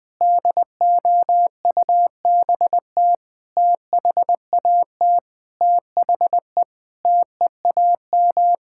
Morse code, heard as a keyed tone, about 21 wpm, 700 Hz. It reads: DOUBT THAT THE TEAM
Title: Sentences with 6-Word Phrases - Spoken Beginning - 20wpm